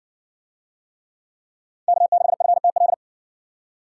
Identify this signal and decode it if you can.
Morse code, heard as a keyed tone, about 60 wpm, 700 Hz. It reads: K6FTL